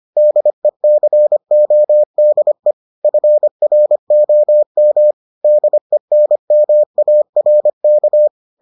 Morse code, heard as a keyed tone, about 25 wpm, 600 Hz. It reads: DECODE FROM DENMARK